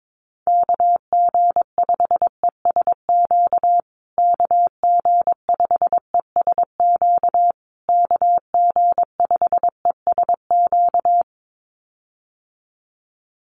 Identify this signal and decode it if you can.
Morse code, heard as a keyed tone, about 22 wpm, 700 Hz. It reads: KG5ESQ KG5ESQ KG5ESQ
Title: USA Call Signs Repeated 3x 22wpm